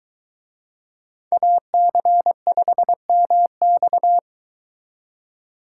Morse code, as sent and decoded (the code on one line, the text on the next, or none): .- -.-. ..... -- -..-
AC5MX